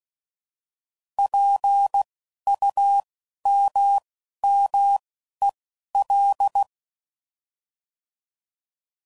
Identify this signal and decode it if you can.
Morse code, heard as a keyed tone, about 16 wpm, spaced as for 11 wpm, 800 Hz. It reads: PUMMEL
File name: Prefixes / By First Letter / Ps